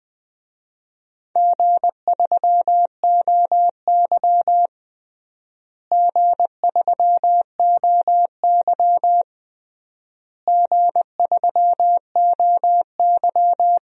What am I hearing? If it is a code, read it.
Morse code, heard as a keyed tone, about 20 wpm, 700 Hz. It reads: G3OY G3OY G3OY
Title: DXCC-100 Call Signs - 4 Characters - Repeated 3x - 20wpm